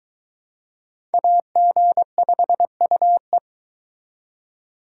Morse code, heard as a keyed tone, about 21 wpm, 700 Hz. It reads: AG5UE